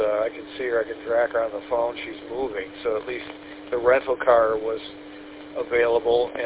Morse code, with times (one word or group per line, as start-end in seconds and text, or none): none